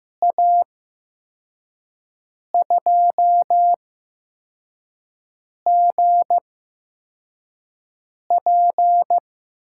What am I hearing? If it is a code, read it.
Morse code, heard as a keyed tone, about 15 wpm, spaced as for 4 wpm, 700 Hz. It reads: A2GP
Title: USA Call Signs - 8x Character Spacing - 15wpm